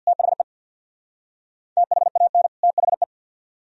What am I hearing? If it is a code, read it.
Morse code, heard as a keyed tone, about 62 wpm, 700 Hz. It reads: THE THAN THE
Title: Speed-Racing - Sets of 2 Words 50wpm